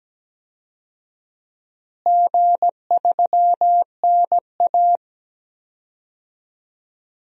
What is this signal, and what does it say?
Morse code, heard as a keyed tone, about 17 wpm, 700 Hz. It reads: G3NA
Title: DXCC-100 Call Signs - 4 Characters - Repeated 2x - 17wpm